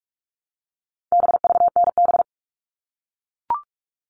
Morse code, heard as a keyed tone, about 44 wpm, 700 Hz. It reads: BVNB
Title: All Cut Numbers - 4-Digit 45wpm